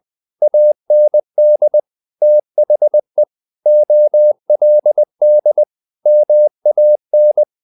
Morse code, heard as a keyed tone, about 20 wpm, 600 Hz. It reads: AND THE OLD MAN
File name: the old man and the sea / the old man and the sea part 1 20wpm